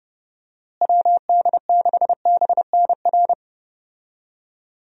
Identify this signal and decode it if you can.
Morse code, heard as a keyed tone, about 30 wpm, 700 Hz. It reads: WD6BNR
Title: USA Call Signs - 6 Characters - Repeated 3x - 30wpm